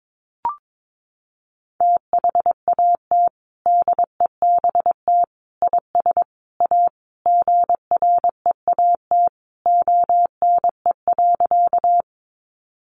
Morse code, heard as a keyed tone, about 22 wpm, 700 Hz. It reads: THAT DEBT IS A GREAT ONE.
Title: Sentences from Top 1300 Words 22wpm